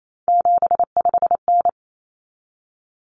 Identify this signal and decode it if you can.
Morse code, heard as a keyed tone, about 28 wpm, 700 Hz. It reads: 75N
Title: Common Cut Numbers - 3-Digit 28wpm